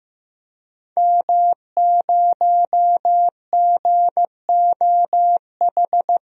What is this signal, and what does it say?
Morse code, heard as a keyed tone, about 15 wpm, 700 Hz. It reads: M0GOH
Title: DXCC-100 Call Signs - 5 Characters - Repeated 2x - 15wpm